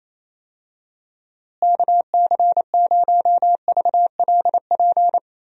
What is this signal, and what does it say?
Morse code, heard as a keyed tone, about 27 wpm, 700 Hz. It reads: KC0VLP
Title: USA Call Signs - 6 Characters - Repeated 2x - 28wpm